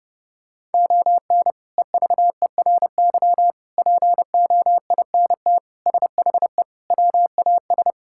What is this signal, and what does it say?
Morse code, heard as a keyed tone, about 30 wpm, 700 Hz. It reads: ON EVERY POINT SHE WAS
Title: Sentences from Top 1000 Words - Review Part 6 - 30wpm